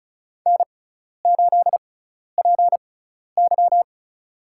Morse code, heard as a keyed tone, about 34 wpm, spaced as for 12 wpm, 700 Hz. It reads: N8PY
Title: USA Call Signs - 4 Characters - 6x Character Spacing - 35wpm